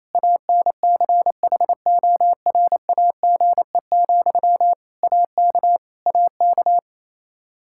Morse code, heard as a keyed tone, about 28 wpm, 700 Hz. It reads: ANCHORAGE, AK AK
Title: Practice QSO - Exchange 1 - Beginning - 28wpm